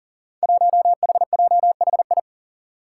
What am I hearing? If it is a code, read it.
Morse code, heard as a keyed tone, about 40 wpm, 700 Hz. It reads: 1HJHI
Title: Five Challenging Letter-Number Combinations 40wpm